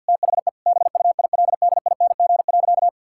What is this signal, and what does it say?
Morse code, heard as a keyed tone, about 50 wpm, 700 Hz. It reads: THE BUILDING.
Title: Sentences from Top 1000 Words - Review Part 5 - 50wpm